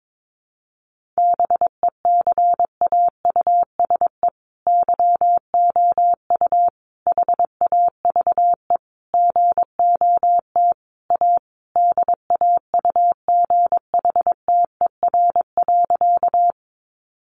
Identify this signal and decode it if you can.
Morse code, heard as a keyed tone, about 22 wpm, 700 Hz. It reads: BECAUSE YOU HAVE GOT A DAUGHTER.
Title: Sentences from Top 1500 Words 22wpm